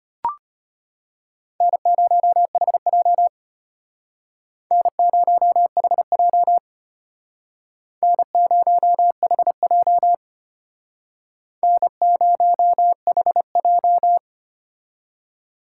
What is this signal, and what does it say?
Morse code, heard as a keyed tone, about 32 wpm, 700 Hz. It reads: N0HJ N0HJ N0HJ N0HJ